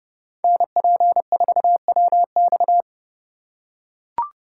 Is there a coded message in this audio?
Yes: Morse code from 0.4 to 2.8 s, heard as a keyed tone, about 27 wpm, 700 Hz. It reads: NP4WX